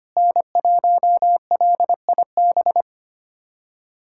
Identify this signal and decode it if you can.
Morse code, heard as a keyed tone, about 25 wpm, 700 Hz. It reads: N1LIB
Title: USA Callsign DE Callsign - 5 Characters - 25wpm